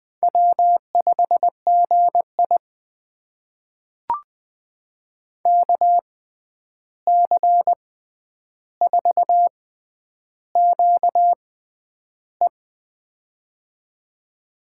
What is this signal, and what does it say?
Morse code, heard as a keyed tone, about 21 wpm, spaced as for 10 wpm, 700 Hz. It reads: W5GI KC4QE
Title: USA Call Signs - 6x Character Spacing - 20wpm